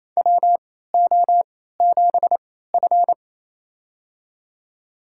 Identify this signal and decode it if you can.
Morse code, heard as a keyed tone, about 28 wpm, spaced as for 16 wpm, 700 Hz. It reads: WO7F
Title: USA Call Signs - 4 Characters - 3x Character Spacing - 28wpm